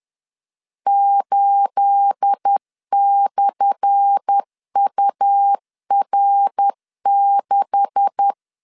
Morse code, heard as a keyed tone, about 10 wpm, 800 Hz. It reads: 8/UR6
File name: LEZIONI / 10 SIGNS / 10A SLASH 10wpm 2808